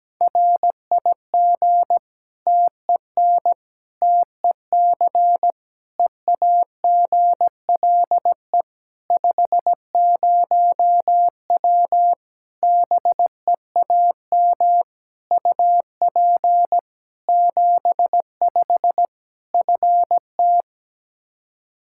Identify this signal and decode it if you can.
Morse code, heard as a keyed tone, about 17 wpm, 700 Hz. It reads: RIG TEN TEC EAGLE 50W BEAM UP 75 FT
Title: Practice QSO - Exchange 2 - Info - 17wpm